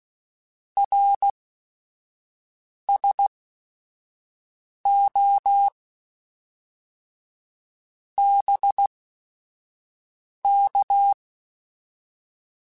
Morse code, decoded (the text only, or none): RSO BK